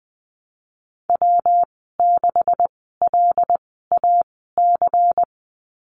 Morse code, heard as a keyed tone, about 20 wpm, spaced as for 15 wpm, 700 Hz. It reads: W6LAC
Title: 179 Lesson 29g - Slash - Call Signs 20wpm